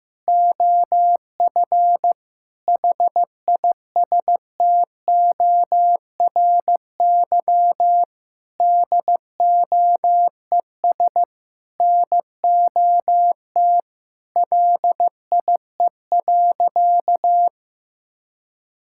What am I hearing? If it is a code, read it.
Morse code, heard as a keyed tone, about 15 wpm, 700 Hz. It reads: OF HISTORY DOES NOT LIE.
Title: Sentences from Top 1600 Words 15wpm